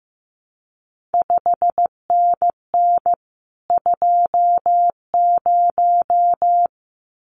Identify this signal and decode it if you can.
Morse code, heard as a keyed tone, about 15 wpm, 700 Hz. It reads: 5NN 20
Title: CQ WW DX Contest Exchange Encore - 15wpm